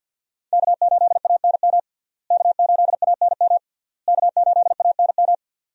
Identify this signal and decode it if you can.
Morse code, heard as a keyed tone, about 52 wpm, 690 Hz. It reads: K8ANM K8ANM K8ANM
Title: USA Call Signs - 5 Characters - Repeated 3x - 50wpm